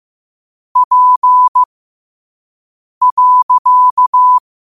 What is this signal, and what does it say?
Morse code, heard as a keyed tone, about 15 wpm, spaced as for 5 wpm, 1000 Hz. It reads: P.
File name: qotdfiles / 5wpm